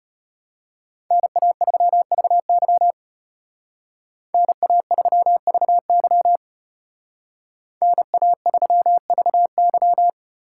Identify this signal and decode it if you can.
Morse code, heard as a keyed tone, about 33 wpm, 700 Hz. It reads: NA3VY NA3VY NA3VY